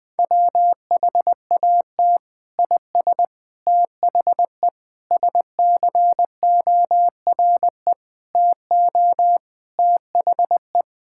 Morse code, heard as a keyed tone, about 20 wpm, 700 Hz. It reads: WHAT IS THE SCORE TO THE
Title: Sentences from Top 1600 Words 20wpm